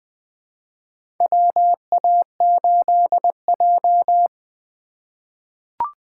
Morse code, heard as a keyed tone, about 20 wpm, 700 Hz. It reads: WA8J
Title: USA Call Signs - 4 Characters - 4x Character Spacing - 20wpm